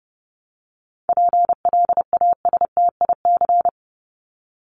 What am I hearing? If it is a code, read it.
Morse code, heard as a keyed tone, about 30 wpm, 700 Hz. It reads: PLASTIC
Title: Top 1001-1250 Words 30wpm